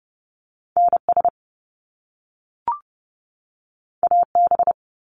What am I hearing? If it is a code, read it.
Morse code, heard as a keyed tone, about 30 wpm, 700 Hz. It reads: NS AB
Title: Canadian Provinces and Territories 30wpm